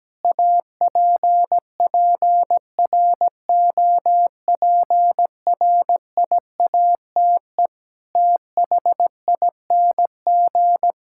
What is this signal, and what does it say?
Morse code, heard as a keyed tone, about 17 wpm, 700 Hz. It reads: APPROPRIATE THING
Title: Sentences from Top 1000 Words - Review Part 5 - 17wpm